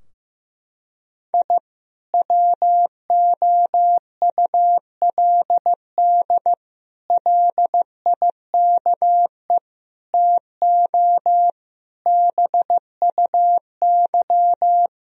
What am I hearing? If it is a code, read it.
Morse code, heard as a keyed tone, about 15 wpm, 700 Hz. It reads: I WOULD LIKE TO BUY
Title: Sentences from Top 1000 Words - Review Part 4 - 15wpm